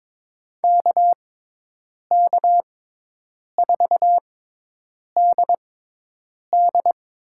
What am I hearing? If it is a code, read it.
Morse code, heard as a keyed tone, about 22 wpm, spaced as for 8 wpm, 700 Hz. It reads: KK4DD
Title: USA Call Signs - 6x Character Spacing - 22wpm